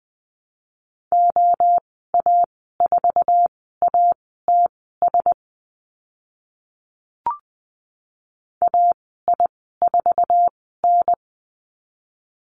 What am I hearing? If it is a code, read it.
Morse code, heard as a keyed tone, about 20 wpm, spaced as for 14 wpm, 700 Hz. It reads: OA4ATS AI4N
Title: 062 Lesson 11g - Letter h - Call Signs 20wpm